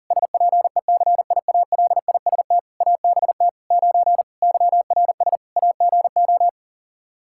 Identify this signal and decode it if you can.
Morse code, heard as a keyed tone, about 37 wpm, 700 Hz. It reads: SPECIALIST ABT 9 YRS AGO
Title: Practice QSO - Exchange 3 - More Info - 40wpm